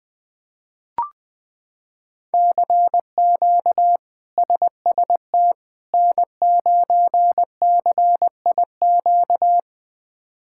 Morse code, heard as a keyed tone, about 21 wpm, 700 Hz. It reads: CQ SST N9CIQ
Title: K1USN SST QSO Practice - 20wpm